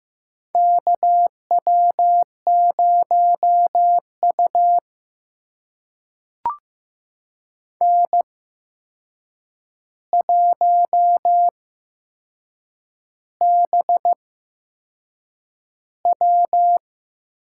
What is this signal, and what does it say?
Morse code, heard as a keyed tone, about 15 wpm, spaced as for 6 wpm, 700 Hz. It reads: KW0U N1BW
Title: USA Call Signs - 4 Characters - 8x Character Spacing - 15wpm